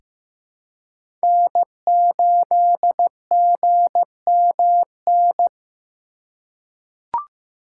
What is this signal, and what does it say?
Morse code, heard as a keyed tone, about 15 wpm, 700 Hz. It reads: N8GMN
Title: USA Call Signs - 5 Characters - 3x Character Spacing - 15wpm